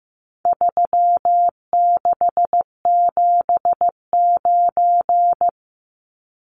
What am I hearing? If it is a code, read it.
Morse code, heard as a keyed tone, about 15 wpm, 700 Hz. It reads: 3679
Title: Numbers - 4-Digit 15wpm